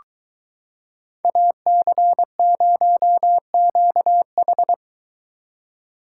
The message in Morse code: AC0QH

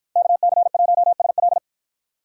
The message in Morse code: KC1SL

